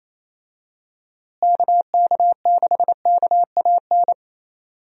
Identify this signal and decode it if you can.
Morse code, heard as a keyed tone, about 28 wpm, 700 Hz. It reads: KK6KAN
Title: USA Call Signs - 6 Characters - Repeated 2x - 28wpm